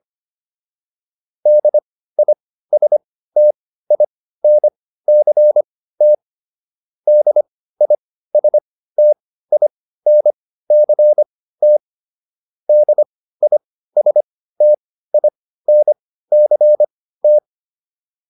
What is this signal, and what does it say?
Morse code, heard as a keyed tone, about 25 wpm, spaced as for 15 wpm, 600 Hz. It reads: DISTINCT DISTINCT DISTINCT